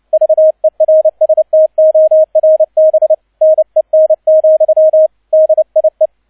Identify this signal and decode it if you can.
Morse code, heard as a keyed tone, about 29 wpm, 630 Hz. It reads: VERSTORBNEN, DIE
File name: 002 Zweiter Streich 30wpm